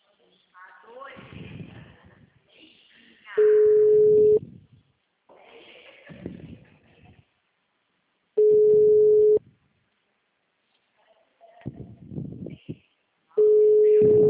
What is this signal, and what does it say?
Ringback tone (Europe, 425 Hz), with speech.
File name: out- -2021- - - .691